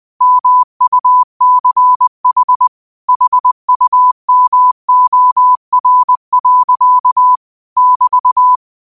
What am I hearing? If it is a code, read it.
Morse code, heard as a keyed tone, about 20 wpm, 1000 Hz. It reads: MUCH HUMOR. <BT>
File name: qotdfiles / 20wpm